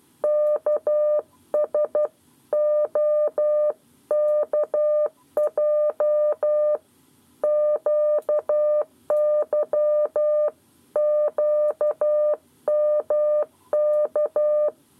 Morse code, heard as a keyed tone, about 12 wpm, 600 Hz. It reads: KSOKJ QYQMK
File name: 7 Lezione cw 1